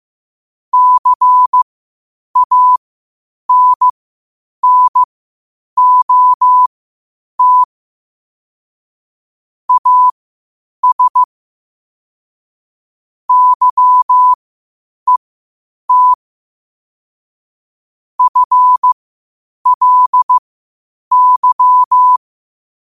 Morse code, heard as a keyed tone, about 15 wpm, spaced as for 8 wpm, 1000 Hz. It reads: CANNOT AS YET FLY
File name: qotdfiles / 8wpm